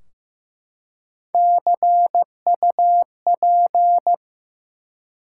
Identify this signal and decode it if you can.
Morse code, heard as a keyed tone, about 15 wpm, 700 Hz. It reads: CUP
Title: Three Letter Words - 15wpm